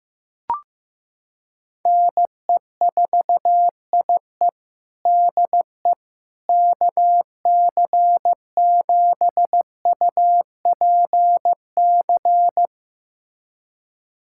Morse code, heard as a keyed tone, about 15 wpm, 700 Hz. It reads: NE4IE DE KC7UPC
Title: USA Callsign DE Callsign 15wpm